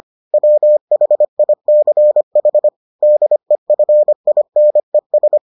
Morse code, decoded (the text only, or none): WHICH DEFINES